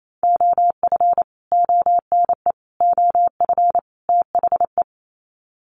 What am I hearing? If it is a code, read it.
Morse code, heard as a keyed tone, about 28 wpm, 700 Hz. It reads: OF ONE OF THE
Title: Sets of 4 Words 28wpm